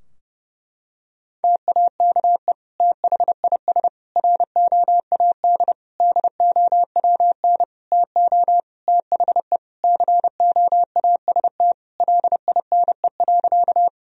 Morse code, heard as a keyed tone, about 30 wpm, 700 Hz. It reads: TAKE THIS ROAD DOWN TO THE COAST LINE.
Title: Sentences from Top 1000 Words - Review Part 4 - 30wpm